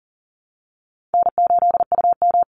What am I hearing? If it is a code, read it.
Morse code, heard as a keyed tone, about 40 wpm, 700 Hz. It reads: N8UM
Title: Call Signs Repeated 2x - 40wpm